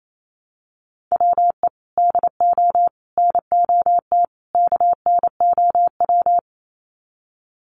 Morse code, heard as a keyed tone, about 28 wpm, 700 Hz. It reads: WE DO NOT KNOW
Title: Sets of 4 Words 28wpm